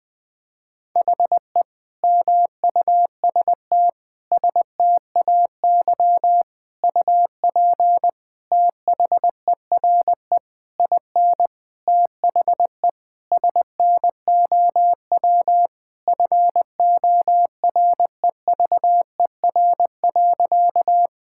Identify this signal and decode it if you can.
Morse code, heard as a keyed tone, about 20 wpm, 700 Hz. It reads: HE MUST STAY UP THERE IN THE SNOW FOREVER.
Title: Sentences from Top 1600 Words 20wpm